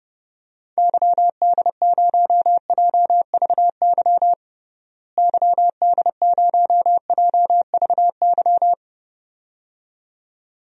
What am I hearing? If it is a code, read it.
Morse code, heard as a keyed tone, about 32 wpm, 700 Hz. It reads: YD0JVY YD0JVY
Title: DXCC-100 Call Signs - 6 Characters - Repeated 2x - 30wpm